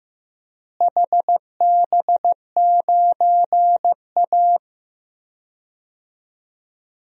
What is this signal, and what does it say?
Morse code, heard as a keyed tone, about 15 wpm, 700 Hz. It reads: HB9A
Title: DXCC-100 Call Signs - 4 Characters - Repeated 2x - 15wpm